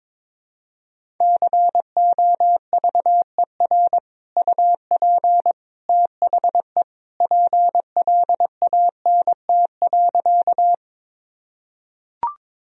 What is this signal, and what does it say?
Morse code, heard as a keyed tone, about 22 wpm, 700 Hz. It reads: COVER UP THE PLANT.
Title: Sentences from Top 1000 Words - Review Part 1 - 22wpm